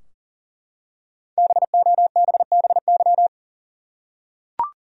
Morse code, heard as a keyed tone, about 40 wpm, 700 Hz. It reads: BOBBY
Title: USA Common Names - Max 5 Letters - 40wpm